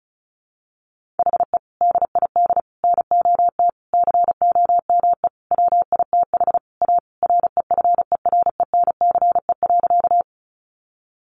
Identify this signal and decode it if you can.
Morse code, heard as a keyed tone, about 34 wpm, 700 Hz. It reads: HE DID NOT COME WITH A REFERENCE.
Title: Sentences from Top 1200 Words 35wpm